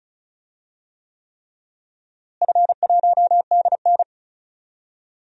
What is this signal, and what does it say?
Morse code, heard as a keyed tone, about 36 wpm, 700 Hz. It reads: F1DN
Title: DXCC-100 Call Signs - 4 Characters - Repeated 3x - 35wpm